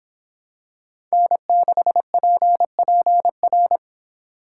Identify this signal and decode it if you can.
Morse code, heard as a keyed tone, about 26 wpm, 700 Hz. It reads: N6PPR